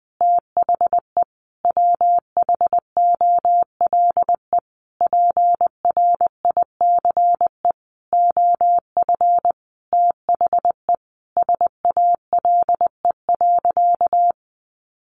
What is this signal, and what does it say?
Morse code, heard as a keyed tone, about 20 wpm, 700 Hz. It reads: THE WHOLE PRICE OF THE SALE.
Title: Sentences from Top 1400 Words 20wpm